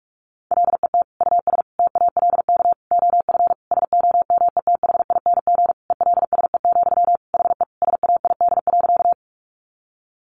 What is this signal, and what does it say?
Morse code, heard as a keyed tone, about 44 wpm, 700 Hz. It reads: LET US TALK OF SOMETHING ELSE, HE SAID.
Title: Sentences from Top 800 Words 45wpm